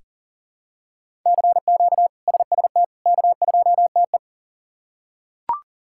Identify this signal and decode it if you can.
Morse code, heard as a keyed tone, about 37 wpm, 700 Hz. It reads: CQ SST K2TE
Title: K1USN SST QSO Practice - 40wpm